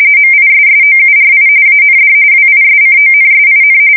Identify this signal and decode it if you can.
RTTY, about 45.45 baud. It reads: MISSOURI AND WISCONSIN
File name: fsk170 / battleshipnj